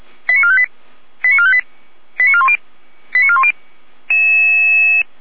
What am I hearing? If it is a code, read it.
Five-tone selective call (ZVEI), heard as quick runs of five tones, tones of about 70 ms. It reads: 89458 89458 89410 89410